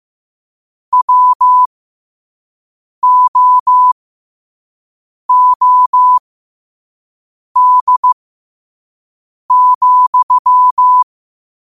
Morse code, heard as a keyed tone, about 15 wpm, spaced as for 5 wpm, 1000 Hz. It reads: WOOD,
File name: qotdfiles / 5wpm